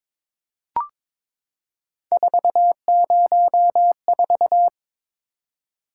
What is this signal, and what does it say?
Morse code, heard as a keyed tone, about 21 wpm, 700 Hz. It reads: 404
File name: practice / 22wpm / Three Letter-Number Combinations - 22wpm